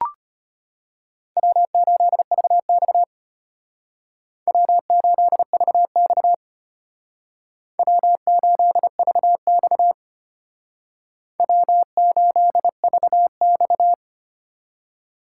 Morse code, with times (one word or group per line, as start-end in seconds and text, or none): 1.4-3.0 W8VX
4.5-6.3 W8VX
7.8-9.9 W8VX
11.4-13.9 W8VX